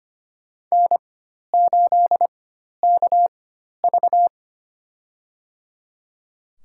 Morse code, heard as a keyed tone, about 25 wpm, spaced as for 12 wpm, 700 Hz. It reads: N8KV
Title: USA Call Signs - 4 Characters - 4x Character Spacing - 25wpm